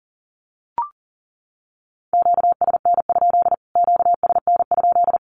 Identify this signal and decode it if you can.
Morse code, heard as a keyed tone, about 40 wpm, 700 Hz. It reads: QSN? QSN?
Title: Q-Codes Repeated 3x 40wpm